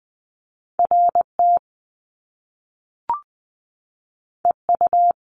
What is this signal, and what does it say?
Morse code, heard as a keyed tone, about 20 wpm, 700 Hz. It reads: R T E U
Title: Top Two Letter Combinations in Words 20wpm